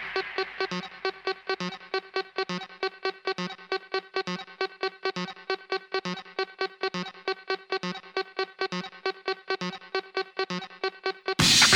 There is no code.